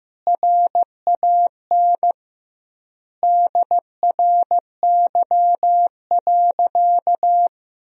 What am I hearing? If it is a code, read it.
Morse code, heard as a keyed tone, about 16 wpm, 700 Hz. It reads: RAN DRY.
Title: 2x Word-Spacing - Sentences from Top 500 Words - 15wpm